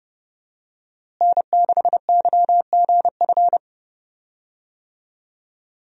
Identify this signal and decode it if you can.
Morse code, heard as a keyed tone, about 30 wpm, 700 Hz. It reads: N6YGF
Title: USA Call Signs - 5 Characters - Repeated 3x - 30wpm